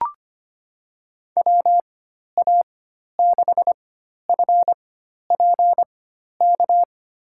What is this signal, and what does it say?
Morse code, heard as a keyed tone, about 25 wpm, spaced as for 12 wpm, 700 Hz. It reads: WA6FPK
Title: USA Call Signs - 6 Characters - 4x Character Spacing - 25wpm